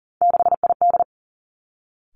Morse code, heard as a keyed tone, about 41 wpm, 690 Hz. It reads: 6ID
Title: Three Challenging Letter-Number Combinations 40wpm